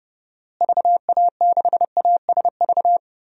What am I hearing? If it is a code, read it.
Morse code, heard as a keyed tone, about 31 wpm, 700 Hz. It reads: VA6ASV
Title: DXCC-100 Call Signs - 6 Characters - Repeated 2x - 30wpm